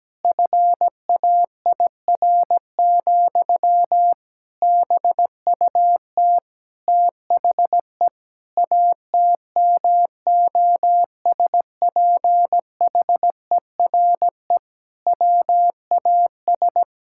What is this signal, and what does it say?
Morse code, heard as a keyed tone, about 17 wpm, 700 Hz. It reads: FAIR, BUT THE ATMOSPHERE WAS
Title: Sentences from Top 1600 Words 17wpm